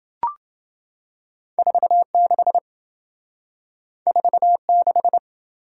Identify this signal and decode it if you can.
Morse code, heard as a keyed tone, about 26 wpm, 700 Hz. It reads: 46 46